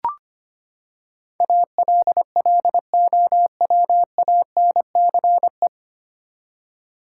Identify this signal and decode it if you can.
Morse code, heard as a keyed tone, about 25 wpm, 700 Hz. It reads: ALLOWANCE
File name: Nine Letter Words - 25wpm